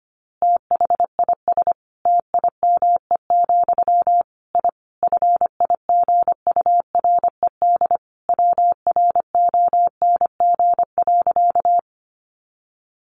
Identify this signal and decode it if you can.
Morse code, heard as a keyed tone, about 25 wpm, 700 Hz. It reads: THIS TIME, I FIGURED WRONG.
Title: A Sound Of Thunder 25wpm